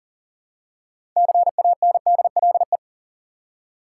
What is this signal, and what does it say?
Morse code, heard as a keyed tone, about 40 wpm, 700 Hz. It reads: CANDLE
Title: Six Letter Words - 40wpm